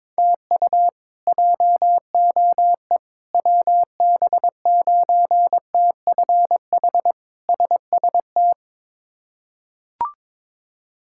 Morse code, heard as a keyed tone, about 23 wpm, 700 Hz. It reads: TU JOE WB9TFH SST
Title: K1USN SST QSO Practice - 22wpm